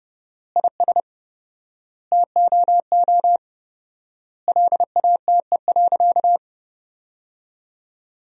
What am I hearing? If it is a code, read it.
Morse code, heard as a keyed tone, about 30 wpm, 700 Hz. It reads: IS TOO LATE.
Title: 4x Word-Spacing - Sentences from Top 300 Words - 30wpm